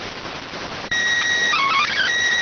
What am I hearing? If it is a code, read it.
SSTV (slow-scan television), PD 290.